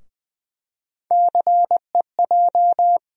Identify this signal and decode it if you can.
Morse code, heard as a keyed tone, about 21 wpm, 700 Hz. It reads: CEJ
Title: USA Call Sign Postfixes - 3 Letters 20wpm